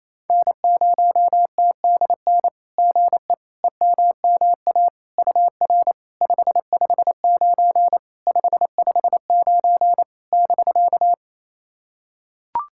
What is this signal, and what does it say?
Morse code, heard as a keyed tone, about 28 wpm, 700 Hz. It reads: N0TDN GE EMMA UR 559 559 <BK>
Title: SOTA QSO Practice 28wpm